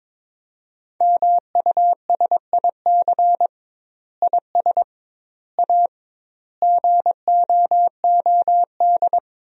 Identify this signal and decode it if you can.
Morse code, heard as a keyed tone, about 22 wpm, 700 Hz. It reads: MUSIC IS A GOOD
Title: 2x Word-Spacing - Sentences from Top 300 Words - 22wpm